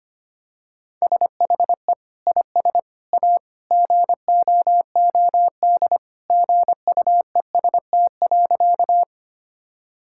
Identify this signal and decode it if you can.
Morse code, heard as a keyed tone, about 25 wpm, 700 Hz. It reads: SHE IS A GOOD GUEST.
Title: Sentences from Top 1600 Words 25wpm